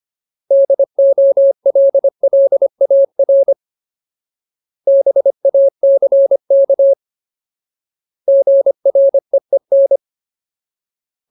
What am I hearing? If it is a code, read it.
Morse code, heard as a keyed tone, about 25 wpm, 550 Hz. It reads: DOLLAR BACK GREEN
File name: Ultimate Word List / 25WPM Ultimate